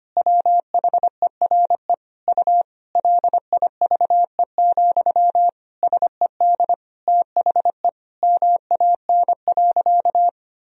Morse code, heard as a keyed tone, about 25 wpm, 700 Hz. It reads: WHERE U LIVE, SED THE MAN.
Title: Sentences from Top 200 Words and Abbreviations 25wpm